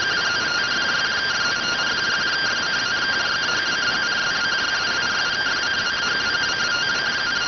Two-tone teleprinter signal, about 170 Hz shift.